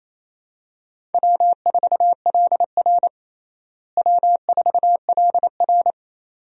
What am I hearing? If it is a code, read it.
Morse code, heard as a keyed tone, about 27 wpm, 700 Hz. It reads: W4LR W4LR
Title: USA Call Signs - 4 Characters - Repeated 2x - 28wpm